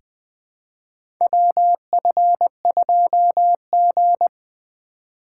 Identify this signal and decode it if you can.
Morse code, heard as a keyed tone, about 20 wpm, 700 Hz. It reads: WF2G
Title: USA Call Signs - 4 Characters - 4x Character Spacing - 20wpm